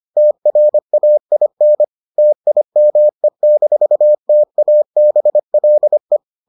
Morse code, heard as a keyed tone, about 25 wpm, 600 Hz. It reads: TRAIN TIME-TABLE